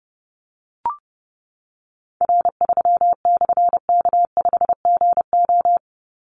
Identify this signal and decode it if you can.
Morse code, heard as a keyed tone, about 30 wpm, 700 Hz. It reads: R3/K5GO
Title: Call Signs - 30wpm